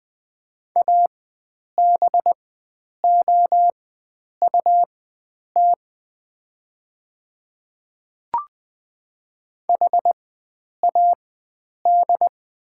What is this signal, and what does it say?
Morse code, heard as a keyed tone, about 20 wpm, spaced as for 9 wpm, 690 Hz. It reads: ABOUT HAD
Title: 4x Character Spacing - Top 300 Words - 20wpm